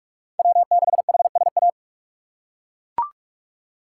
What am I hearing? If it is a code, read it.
Morse code, heard as a keyed tone, about 42 wpm, 700 Hz. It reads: W6HSA
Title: USA Call Signs - 5 Characters - 2x Character Spacing - 45wpm